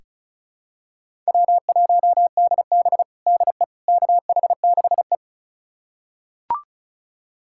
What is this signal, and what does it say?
Morse code, heard as a keyed tone, about 36 wpm, 700 Hz. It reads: W1DB DE KH6E
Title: USA Callsign DE Callsign - 4 Characters - 35wpm